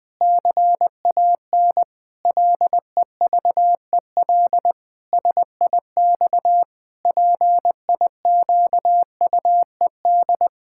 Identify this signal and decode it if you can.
Morse code, heard as a keyed tone, about 19 wpm, 700 Hz. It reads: CAN LEVEL SIX PIQUED
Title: Pangram Challenge v2 - 20wpm